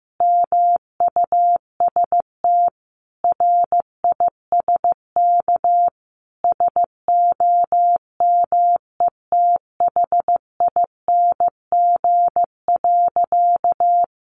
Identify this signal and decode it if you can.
Morse code, heard as a keyed tone, about 15 wpm, 700 Hz. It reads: MUST RISK SOMETHING.
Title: Sentences from Top 600 Words 15wpm